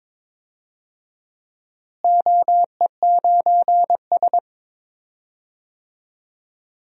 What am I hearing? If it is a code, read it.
Morse code, heard as a keyed tone, about 22 wpm, 700 Hz. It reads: OE9S
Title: DXCC-100 Call Signs - 4 Characters - Repeated 3x - 22wpm